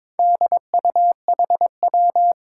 Morse code, heard as a keyed tone, about 22 wpm, 700 Hz. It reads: DUHW
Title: Four Challenging Letter-Number Combinations 22wpm